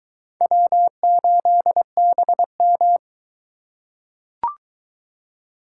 Morse code, heard as a keyed tone, about 22 wpm, 700 Hz. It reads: W8BM